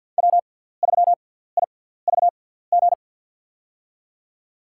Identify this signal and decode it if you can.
Morse code, heard as a keyed tone, about 47 wpm, spaced as for 17 wpm, 700 Hz. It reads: W3IVG
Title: USA Call Signs - 5 Characters - 6x Character Spacing - 50wpm